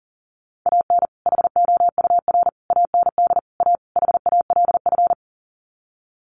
Morse code, heard as a keyed tone, about 40 wpm, 700 Hz. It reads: AN HOUR AND A HALF
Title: Sets of 5 Words 40wpm